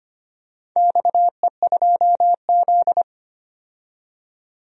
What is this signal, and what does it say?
Morse code, heard as a keyed tone, about 25 wpm, 700 Hz. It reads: XE2Z
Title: DXCC-100 Call Signs - 4 Characters - 25wpm